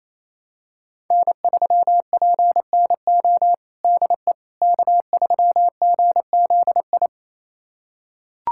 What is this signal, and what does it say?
Morse code, heard as a keyed tone, about 27 wpm, 700 Hz. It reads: N3PNO DE K3GZI
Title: USA Callsign DE Callsign - 5 Characters - 28wpm